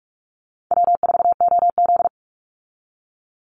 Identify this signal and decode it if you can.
Morse code, heard as a keyed tone, about 44 wpm, 700 Hz. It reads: W4OZ
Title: USA Call Signs 45wpm